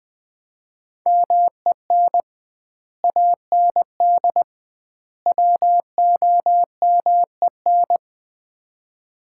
Morse code, heard as a keyed tone, about 21 wpm, 700 Hz. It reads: MEN AND WOMEN
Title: 2x Word-Spacing - Sets of 3 Words - 20wpm